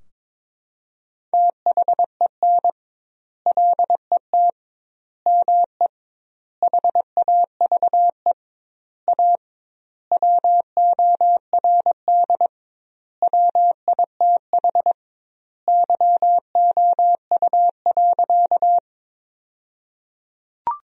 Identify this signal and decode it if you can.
Morse code, heard as a keyed tone, about 22 wpm, 700 Hz. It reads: THEN LET ME HAVE A WORD WITH YOU.
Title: 2x Word-Spacing - Sentences from Top 300 Words - 22wpm